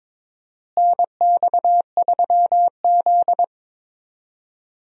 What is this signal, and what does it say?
Morse code, heard as a keyed tone, about 21 wpm, 700 Hz. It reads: NX3Z
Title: USA Call Signs - 4 Characters - Repeated 3x - 22wpm